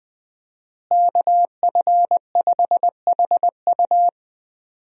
Morse code, heard as a keyed tone, about 20 wpm, 700 Hz. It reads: KF5HU
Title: USA Call Signs - 5 Characters - Repeated 2x - 20wpm